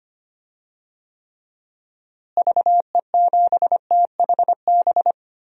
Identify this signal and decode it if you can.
Morse code, heard as a keyed tone, about 25 wpm, 700 Hz. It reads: VE7THB
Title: DXCC-100 Call Signs - 6 Characters - Repeated 2x - 25wpm